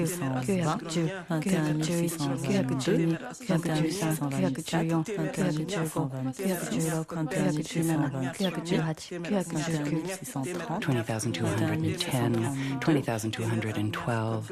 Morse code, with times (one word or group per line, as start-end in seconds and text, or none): none